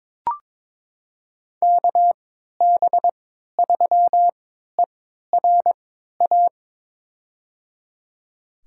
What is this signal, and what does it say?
Morse code, heard as a keyed tone, about 22 wpm, spaced as for 12 wpm, 700 Hz. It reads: KB3ERA
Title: USA Call Signs - 6 Characters - 3x Character Spacing - 22wpm